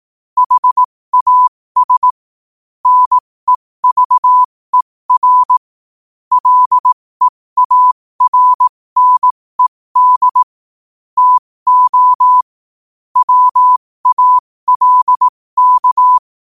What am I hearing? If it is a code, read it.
Morse code, heard as a keyed tone, about 18 wpm, spaced as for 15 wpm, 1000 Hz. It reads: HAS NEVER LEARNED TO WALK
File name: qotdfiles / 15wpm